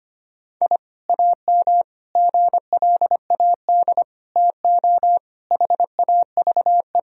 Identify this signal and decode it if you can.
Morse code, heard as a keyed tone, about 25 wpm, 700 Hz. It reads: I AM GLAD TO HAVE
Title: Sentences from Top 1000 Words - Review Part 4 - 25wpm